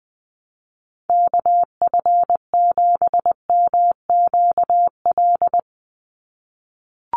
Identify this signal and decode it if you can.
Morse code, heard as a keyed tone, about 20 wpm, 700 Hz. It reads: KF7MQL
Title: USA Call Signs 20wpm